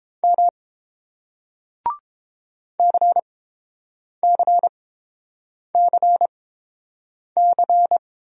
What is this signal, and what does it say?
Morse code, heard as a keyed tone, about 30 wpm, spaced as for 7 wpm, 700 Hz. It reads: M CCCC